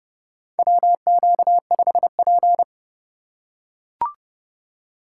Morse code, heard as a keyed tone, about 32 wpm, 700 Hz. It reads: WQ5P
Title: USA Call Signs - 4 Characters - 4x Character Spacing - 30wpm